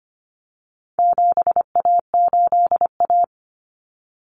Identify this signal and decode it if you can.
Morse code, heard as a keyed tone, about 25 wpm, 700 Hz. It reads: 7A8A
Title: Common Cut Numbers - 4-Digit 25wpm